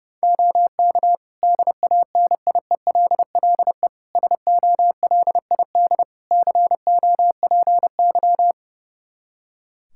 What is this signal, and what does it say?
Morse code, heard as a keyed tone, about 30 wpm, 700 Hz. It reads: OK DANIELLE SOLID COPY
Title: Practice QSO - Exchange 3 - More Info - 30wpm